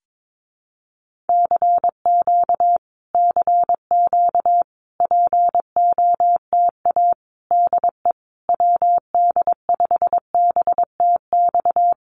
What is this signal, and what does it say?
Morse code, heard as a keyed tone, about 22 wpm, 700 Hz. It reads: CQ CQ POTA DE WD5BTX
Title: Common QSO Phrases Encore 22wpm